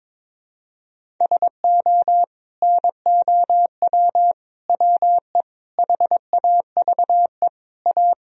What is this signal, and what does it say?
Morse code, heard as a keyed tone, about 22 wpm, 700 Hz. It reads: SO NOW WE HAVE A
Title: Sentences from Top 1000 Words - Review Part 2 - 22wpm